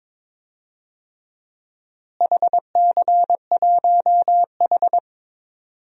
Morse code, heard as a keyed tone, about 21 wpm, 700 Hz. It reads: HC1H
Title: DXCC-100 Call Signs - 4 Characters - Repeated 2x - 22wpm